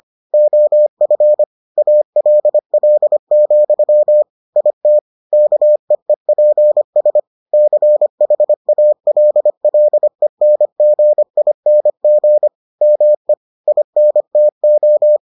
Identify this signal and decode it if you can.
Morse code, heard as a keyed tone, about 25 wpm, 600 Hz. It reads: OF ALL, IT KEEPS CHALLENGING ME INTO